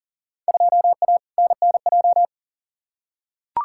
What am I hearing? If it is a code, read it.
Morse code, heard as a keyed tone, about 40 wpm, 700 Hz. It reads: 2A NNJ
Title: Field Day Exchanges - 40wpm